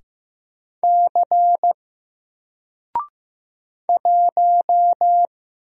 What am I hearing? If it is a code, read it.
Morse code, heard as a keyed tone, about 15 wpm, 700 Hz. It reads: C 1
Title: Single Character (Letter:Number:Punctuation:Prosign) 15wpm